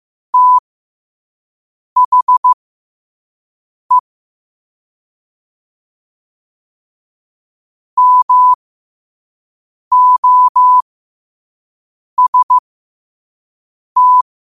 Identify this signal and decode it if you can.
Morse code, heard as a keyed tone, about 15 wpm, spaced as for 5 wpm, 1000 Hz. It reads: THE MOST
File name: qotdfiles / 5wpm